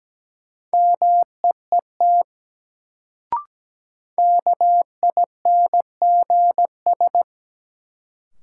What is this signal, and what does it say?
Morse code, heard as a keyed tone, about 17 wpm, 700 Hz. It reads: MEET KINGS
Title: Sentences from Top 1000 Words - Review Part 6 - New Word Drill - 17wpm